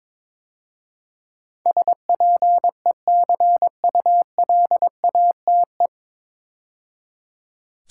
Morse code, heard as a keyed tone, about 22 wpm, 700 Hz. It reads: SPECULATE
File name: 9 Letter Words - First 3 Letters Given - 22wpm